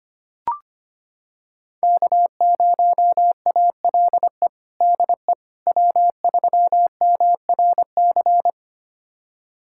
Morse code, heard as a keyed tone, about 25 wpm, 700 Hz. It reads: K0ALE DE W3MRC
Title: USA Callsign DE Callsign - 5 Characters - 25wpm